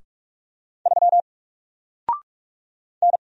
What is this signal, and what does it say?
Morse code, heard as a keyed tone, about 44 wpm, 700 Hz. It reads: 3 N
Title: Single Letter-Number Alt 45wpm